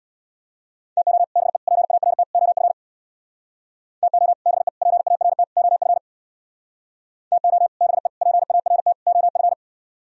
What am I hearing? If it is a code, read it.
Morse code, heard as a keyed tone, about 63 wpm, 700 Hz. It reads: TO BE PART OF TO BE PART OF TO BE PART OF